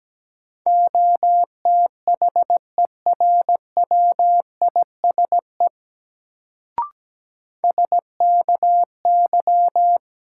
Morse code, heard as a keyed tone, about 17 wpm, 700 Hz. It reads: OTHERWISE SKY
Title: Sentences from Top 1200 Words - New Word Drill - 17wpm